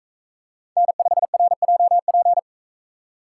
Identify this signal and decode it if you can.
Morse code, heard as a keyed tone, about 40 wpm, 700 Hz. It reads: N5RJP